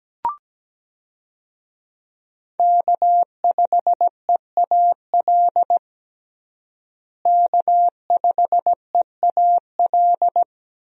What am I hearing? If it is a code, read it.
Morse code, heard as a keyed tone, about 17 wpm, 700 Hz. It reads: K5EAL K5EAL
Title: USA Call Signs - 5 Characters - Repeated 3x - 17wpm